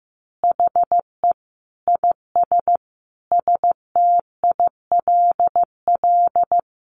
Morse code, heard as a keyed tone, about 15 wpm, 700 Hz. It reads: HE IS STILL
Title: Easy Phrases 15wpm